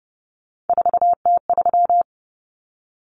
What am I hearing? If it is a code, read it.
Morse code, heard as a keyed tone, about 30 wpm, 700 Hz. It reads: 4T3
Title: Common Cut Numbers - 3-Digit 30wpm